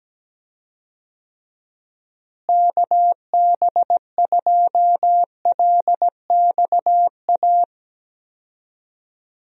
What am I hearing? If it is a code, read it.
Morse code, heard as a keyed tone, about 17 wpm, 700 Hz. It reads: KB2LXA
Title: USA Call Signs - 6 Characters - Repeated 3x - 17wpm